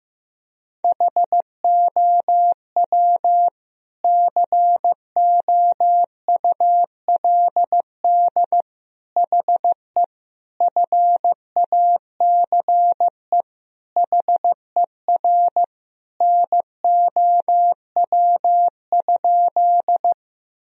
Morse code, heard as a keyed tone, about 15 wpm, 700 Hz. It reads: HOW COULD HE FACE HER NOW?
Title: Sentences from Top 1000 Words - Review Part 2 - 15wpm